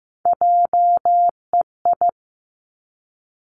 Morse code, heard as a keyed tone, about 15 wpm, 700 Hz. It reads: JEI
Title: Three Challenging Letter-Number Combinations 15wpm